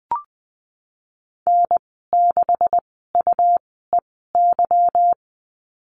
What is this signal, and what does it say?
Morse code, heard as a keyed tone, about 20 wpm, spaced as for 15 wpm, 700 Hz. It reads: N6UEY
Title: 154 Lesson 25g - Letter p - Call Signs 20wpm